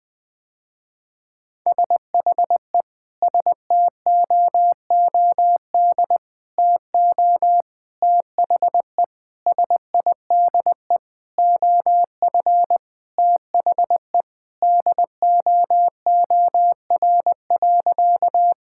Morse code, heard as a keyed tone, about 20 wpm, 700 Hz. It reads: SHE STOOD TO THE SIDE OF THE DOOR.
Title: Sentences with 5-Word Phrases - Spoken Beginning - 20wpm